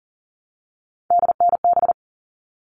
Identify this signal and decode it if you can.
Morse code, heard as a keyed tone, about 40 wpm, 700 Hz. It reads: DNB
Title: All Cut Numbers - 3-Digit 40wpm